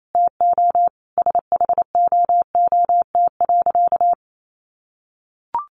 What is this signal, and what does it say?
Morse code, heard as a keyed tone, about 28 wpm, 700 Hz. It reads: TO SHOOT.
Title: A Sound Of Thunder 28wpm